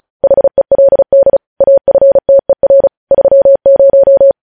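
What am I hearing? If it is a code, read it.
Morse code, heard as a keyed tone, about 36 wpm, 550 Hz. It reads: HELD AFTER 30
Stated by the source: Morse code news headlines at 35.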